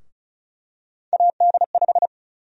Morse code, heard as a keyed tone, about 34 wpm, 700 Hz. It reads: AD5
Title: USA Call Sign Prefixes - 35wpm